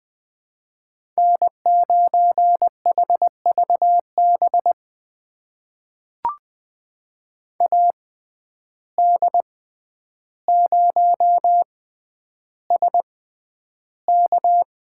Morse code, heard as a keyed tone, about 20 wpm, spaced as for 10 wpm, 700 Hz. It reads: N9HVB AD0SK
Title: USA Call Signs - 5 Characters - 6x Character Spacing - 20wpm